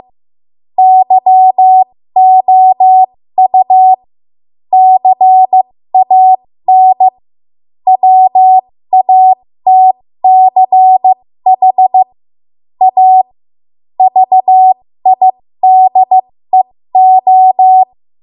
Morse code, heard as a keyed tone, about 15 wpm, spaced as for 13 wpm, 750 Hz. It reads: YOU CAN WATCH A VIDEO
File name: Morse / 13WPM